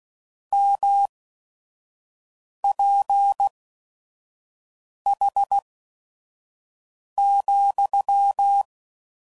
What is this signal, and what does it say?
Morse code, heard as a keyed tone, about 16 wpm, spaced as for 5 wpm, 800 Hz. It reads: MPH,
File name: CW Code Groups @5WPM / Weather Sets / Weather 4 24